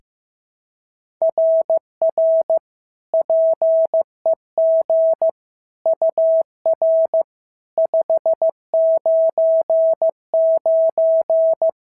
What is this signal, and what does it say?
Morse code, heard as a keyed tone, about 14 wpm, 650 Hz. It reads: RR PEG UR 599
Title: SOTA QSO Practice 15wpm